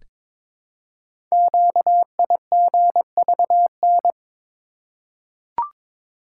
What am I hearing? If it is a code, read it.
Morse code, heard as a keyed tone, about 22 wpm, 700 Hz. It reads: QIGVN
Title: Code Groups - 5 Letters - 22wpm